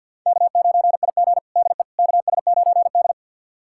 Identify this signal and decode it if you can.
Morse code, heard as a keyed tone, about 46 wpm, 700 Hz. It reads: K9IG DE KS9D
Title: USA Callsign DE Callsign - 4 Characters - 50wpm